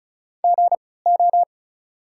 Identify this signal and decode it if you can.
Morse code, heard as a keyed tone, about 36 wpm, spaced as for 20 wpm, 690 Hz. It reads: GO
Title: 3x Character Spacing - Top 300 Words - 35wpm